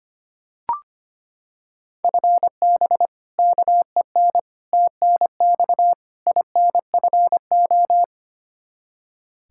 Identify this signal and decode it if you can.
Morse code, heard as a keyed tone, about 25 wpm, 700 Hz. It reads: FB KEN TNX INFO
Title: Practice QSO - Exchange 3 - More Info - 25wpm